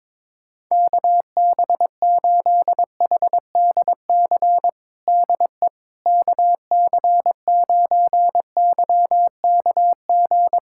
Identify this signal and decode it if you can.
Morse code, heard as a keyed tone, about 22 wpm, 700 Hz. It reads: KB8HDC DE KC9YKG
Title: USA Callsign DE Callsign - 6 Characters - 22wpm